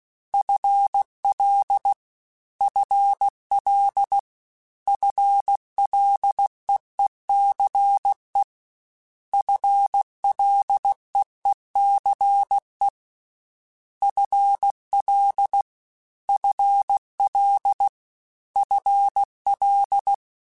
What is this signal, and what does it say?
Morse code, heard as a keyed tone, about 16 wpm, 800 Hz. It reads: FL FL FLEECE FLEECE FL FL FL
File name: FL Prefixes 3x2 @15